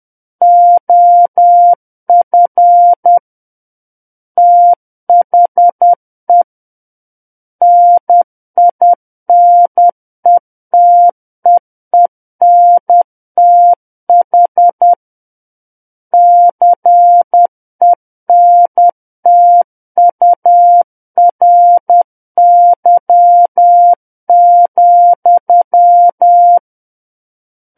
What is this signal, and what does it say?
Morse code, heard as a keyed tone, about 10 wpm, 700 Hz. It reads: OF THE NINETEENTH CENTURY,
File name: Book One / 01-Chapter 1 - 10 WPM